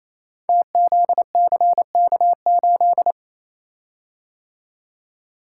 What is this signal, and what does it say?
Morse code, heard as a keyed tone, about 27 wpm, 700 Hz. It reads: TZCK8
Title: Code Groups - 5 Letter:Numbers - 28wpm